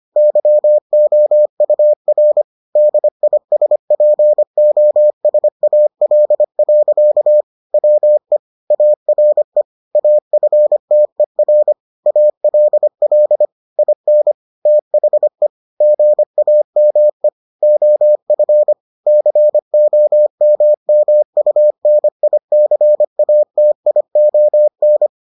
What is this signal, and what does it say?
Morse code, heard as a keyed tone, about 25 wpm, 600 Hz. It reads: YOUR DISPOSAL. WE ARE AFTER ALL IN THE GAME OF COMMUNICATION